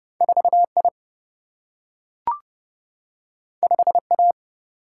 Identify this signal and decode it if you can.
Morse code, heard as a keyed tone, about 30 wpm, 700 Hz. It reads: 4 I 5 A
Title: Single-Letter Single-Number 30wpm